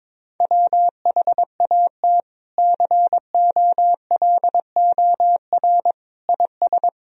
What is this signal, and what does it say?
Morse code, heard as a keyed tone, about 22 wpm, 700 Hz. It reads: WHAT COLOR IS
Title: Sentences from Top 1000 Words - Review Part 2 - 22wpm